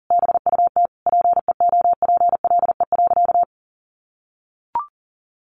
Morse code, heard as a keyed tone, about 40 wpm, 700 Hz. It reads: BUT PEOPLE.
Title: Sentences from Top 1400 Words 40wpm